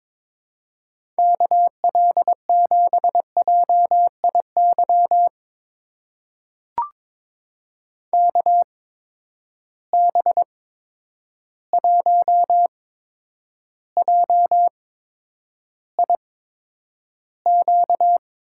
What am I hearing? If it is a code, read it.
Morse code, heard as a keyed tone, about 21 wpm, spaced as for 9 wpm, 700 Hz. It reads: KL7JIY KB1JIQ
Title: USA Call Signs - 6 Characters - 8x Character Spacing - 22wpm